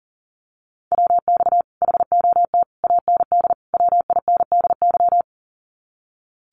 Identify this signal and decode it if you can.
Morse code, heard as a keyed tone, about 41 wpm, 700 Hz. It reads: WX HOT AND WINDY
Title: Common QSO Phrases Encore 40wpm